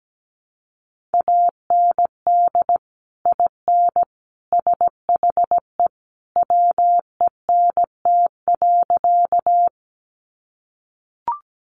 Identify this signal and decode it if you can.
Morse code, heard as a keyed tone, about 17 wpm, 700 Hz. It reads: AND IN SHE WENT.
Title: Sentences from Top 200 Words 17wpm